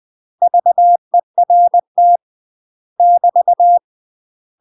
Morse code, heard as a keyed tone, about 20 wpm, 700 Hz. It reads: VERT <BT>
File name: UK MASTER QSOs TRAINING / 20WPM / UK MASTER QSOs 0066